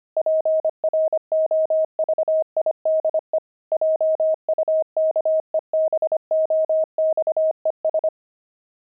Morse code, heard as a keyed tone, about 25 wpm, 630 Hz. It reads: PROVIDE JUKEBOXES
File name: pangram-fs-25-cs-25-disco